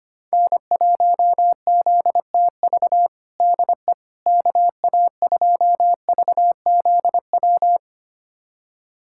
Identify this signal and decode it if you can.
Morse code, heard as a keyed tone, about 25 wpm, 700 Hz. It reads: N1ZTV DE KA2VZW
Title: USA Callsign DE Callsign 25wpm